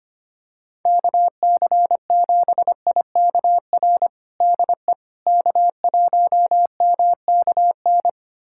Morse code, heard as a keyed tone, about 25 wpm, 700 Hz. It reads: KC7IKR DE K1MKN
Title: USA Callsign DE Callsign 25wpm